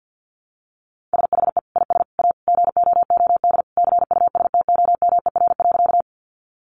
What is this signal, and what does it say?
Morse code, heard as a keyed tone, about 50 wpm, 700 Hz. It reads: SHE IS A GOOD CUSTOMER.
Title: Sentences from Top 1100 Words 50wpm